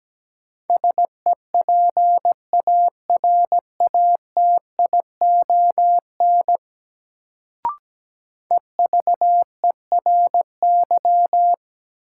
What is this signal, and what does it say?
Morse code, heard as a keyed tone, about 17 wpm, 700 Hz. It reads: SEPARATION EVERY
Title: Sets of 5 Words - New Word Drill - 17wpm